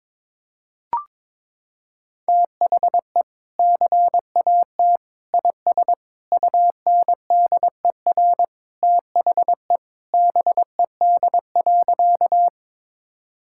Morse code, heard as a keyed tone, about 22 wpm, 700 Hz. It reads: THE CAT IS UNDER THE BED.
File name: practice / 22wpm / Sentences with 2-Word Phrases - 22wpm